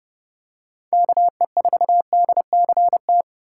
Morse code, heard as a keyed tone, about 30 wpm, 700 Hz. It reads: KE4DCT
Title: USA Call Signs - 2x Character Spacing - 30wpm